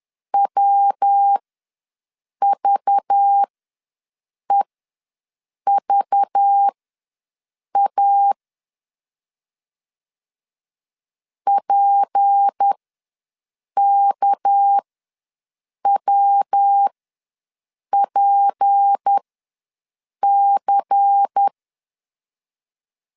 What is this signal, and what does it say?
Morse code, heard as a keyed tone, about 10 wpm, spaced as for 6 wpm, 800 Hz. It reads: WVEVA PKWPC
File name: LEZIONI / 6 PKX0 / 6 PK ws 10wpm 3745